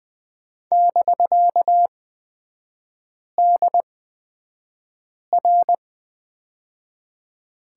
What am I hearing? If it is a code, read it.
Morse code, heard as a keyed tone, about 20 wpm, spaced as for 5 wpm, 700 Hz. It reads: <BK>DR
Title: 250 Lesson 40e - BK - Three Characters 20wpm